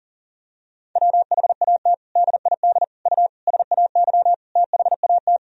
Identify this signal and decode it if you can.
Morse code, heard as a keyed tone, about 40 wpm, 700 Hz. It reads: WHAT DID U SAY THAT
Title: Sentences from Top 200 Words and Abbreviations 40wpm